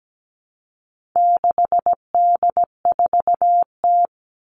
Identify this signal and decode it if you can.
Morse code, heard as a keyed tone, about 17 wpm, 700 Hz. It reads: 6D4T
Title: All Cut Numbers - 4-Digit 17wpm